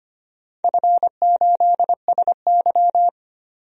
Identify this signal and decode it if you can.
Morse code, heard as a keyed tone, about 25 wpm, 700 Hz. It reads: F8SY
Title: DXCC-100 Call Signs - 4 Characters - 25wpm